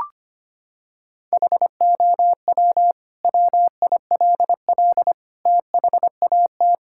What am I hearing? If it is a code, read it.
Morse code, heard as a keyed tone, about 25 wpm, 700 Hz. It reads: HOW WILL THAT
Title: Sentences from Top 1000 Words - Review Part 6 - 25wpm